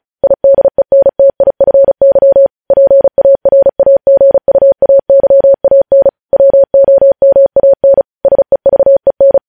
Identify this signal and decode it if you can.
Morse code, heard as a keyed tone, about 36 wpm, 550 Hz. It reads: IDENTIFY PARAGUAYAN WOMAN SEVEN
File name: podcast / News headlines-1x-35x35